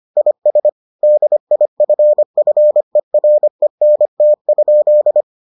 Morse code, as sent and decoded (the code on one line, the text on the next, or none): .. ... / -.. .. ..-. ..-. .-. -. - ..--..
IS DIFFERENT?